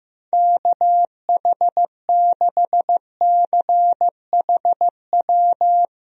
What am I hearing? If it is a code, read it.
Morse code, heard as a keyed tone, about 15 wpm, 700 Hz. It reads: KH6CHW
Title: USA Call Signs - 6 Characters - 15wpm